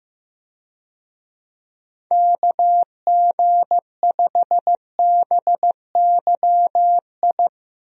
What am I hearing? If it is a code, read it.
Morse code, heard as a keyed tone, about 15 wpm, 700 Hz. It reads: KG5BYI
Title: USA Call Signs - 6 Characters - Repeated 3x - 15wpm